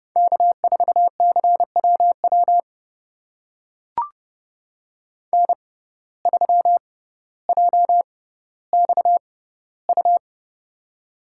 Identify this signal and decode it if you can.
Morse code, heard as a keyed tone, about 30 wpm, spaced as for 15 wpm, 700 Hz. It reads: K4CWW N3JXU
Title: USA Call Signs - 5 Characters - 6x Character Spacing - 30wpm